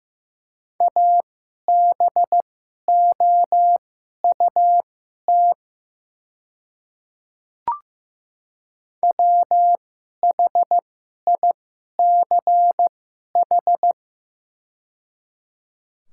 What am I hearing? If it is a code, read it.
Morse code, heard as a keyed tone, about 15 wpm, spaced as for 11 wpm, 700 Hz. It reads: ABOUT WHICH
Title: 2x Character Spacing - Top 200 Words - 15wpm